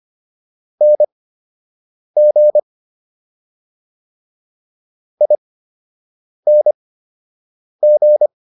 Morse code, heard as a keyed tone, about 25 wpm, spaced as for 7 wpm, 600 Hz. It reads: NG ING